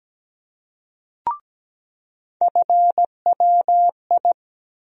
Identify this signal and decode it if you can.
Morse code, heard as a keyed tone, about 17 wpm, 700 Hz. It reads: FWI
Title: USA Call Sign Postfixes - 3 Letters 17wpm